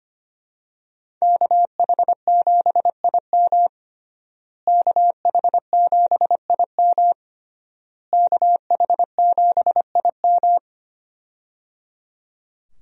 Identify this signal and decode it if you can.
Morse code, heard as a keyed tone, about 25 wpm, 700 Hz. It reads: KH7IM KH7IM KH7IM
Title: USA Call Signs - 5 Characters - Repeated 3x - 25wpm